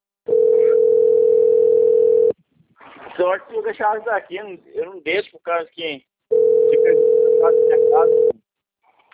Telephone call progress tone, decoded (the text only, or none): ringback tone